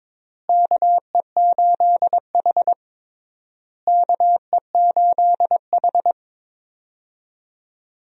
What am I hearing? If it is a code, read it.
Morse code, heard as a keyed tone, about 22 wpm, 700 Hz. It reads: KE8H KE8H
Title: USA Call Signs - 4 Characters - Repeated 3x - 22wpm